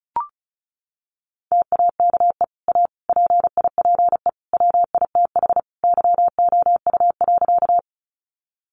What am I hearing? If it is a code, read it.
Morse code, heard as a keyed tone, about 34 wpm, 700 Hz. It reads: TAKE A PIPE WITH YOU.
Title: Sentences from Top 1500 Words 35wpm